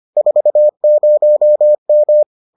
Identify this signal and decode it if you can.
Morse code, heard as a keyed tone, about 25 wpm, 600 Hz. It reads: 40M